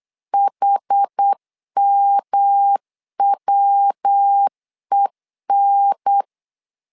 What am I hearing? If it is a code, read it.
Morse code, heard as a keyed tone, about 8 wpm, 800 Hz. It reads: HMWEN
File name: LEZIONI / 4 Q1 8wpm 3035